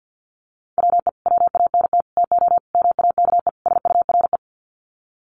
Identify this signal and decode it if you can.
Morse code, heard as a keyed tone, about 50 wpm, 700 Hz. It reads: WE WANT TO MAKE SURE
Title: Sets of 5 Words Encore2 50wpm